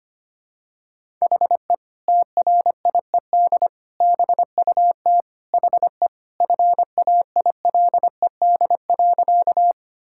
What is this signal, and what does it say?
Morse code, heard as a keyed tone, about 25 wpm, 700 Hz. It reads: HE TRIED BUT HE FAILED.
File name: practice / 25wpm / Sentences with 2-Word Phrases - 25wpm